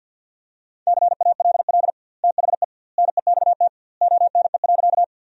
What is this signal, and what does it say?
Morse code, heard as a keyed tone, about 52 wpm, 700 Hz. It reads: CALL THE NEXT ONE.
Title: 2x Word-Spacing - Sentences from Top 300 Words - 50wpm